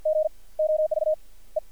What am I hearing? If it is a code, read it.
Morse code, heard as a keyed tone, about 55 wpm, spaced as for 32 wpm, 620 Hz. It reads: GOV E
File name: archived / CRSnet